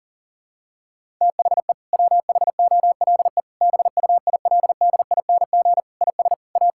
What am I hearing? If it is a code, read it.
Morse code, heard as a keyed tone, about 40 wpm, 700 Hz. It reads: THE WHOLE BUILDING IS A
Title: Sentences from Top 1000 Words - Review Part 5 - 40wpm